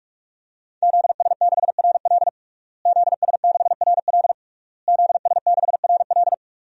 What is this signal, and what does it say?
Morse code, heard as a keyed tone, about 46 wpm, 700 Hz. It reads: ZS6RL ZS6RL ZS6RL
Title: DXCC-100 Call Signs - 5 Characters - Repeated 3x - 45wpm